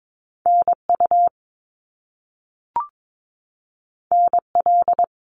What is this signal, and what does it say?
Morse code, heard as a keyed tone, about 22 wpm, 700 Hz. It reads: NU NL
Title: Canadian Provinces and Territories 22wpm